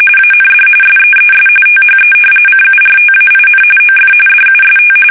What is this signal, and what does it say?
RTTY, about 45.45 baud. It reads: AND SEES THE CURTAIN OF DUST
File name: fsk850 / battleshipnj